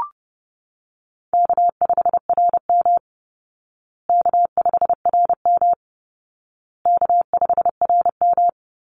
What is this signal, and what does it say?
Morse code, heard as a keyed tone, about 30 wpm, 700 Hz. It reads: K5RM K5RM K5RM
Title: Easy Call Signs Repeated 3x - 30wpm